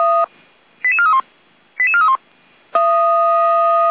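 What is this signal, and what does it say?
Five-tone selective call (ZVEI), heard as quick runs of five tones, tones of about 70 ms. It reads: 88431 88431
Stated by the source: ZVEI five-tone selective call received on SDR, 455 channel path.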